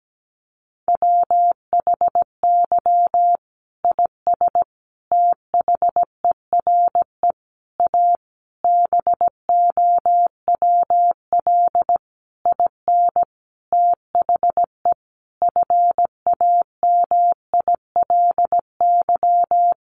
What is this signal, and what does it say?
Morse code, heard as a keyed tone, about 17 wpm, 700 Hz. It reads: WHY IS THERE A BOWL IN THE FAMILY
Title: Sentences from Top 1200 Words 17wpm